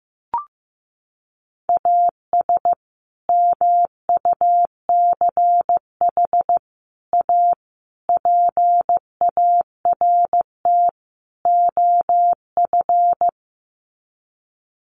Morse code, heard as a keyed tone, about 15 wpm, 700 Hz. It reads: AS MUCH A PART OF
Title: Sets of 5 Words Encore2 15wpm